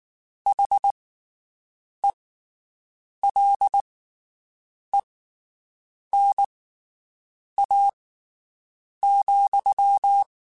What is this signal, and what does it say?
Morse code, heard as a keyed tone, about 19 wpm, spaced as for 7 wpm, 800 Hz. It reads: HELENA,
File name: (spaced) / QSOs Spaced-mp3s / WD0BGP 1 @18 sp